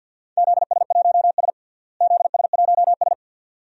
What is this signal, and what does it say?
Morse code, heard as a keyed tone, about 53 wpm, 700 Hz. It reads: ZS1S ZS1S
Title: DXCC-100 Call Signs - 4 Characters - Repeated 2x - 50wpm